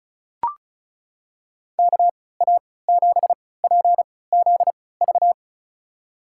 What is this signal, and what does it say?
Morse code, heard as a keyed tone, about 36 wpm, spaced as for 20 wpm, 700 Hz. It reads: KA7PZV
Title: USA Call Signs - 6 Characters - 3x Character Spacing - 35wpm